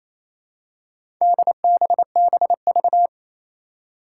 Morse code, heard as a keyed tone, about 27 wpm, 700 Hz. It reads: DBBV
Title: Four Challenging Letter-Number Combinations 28wpm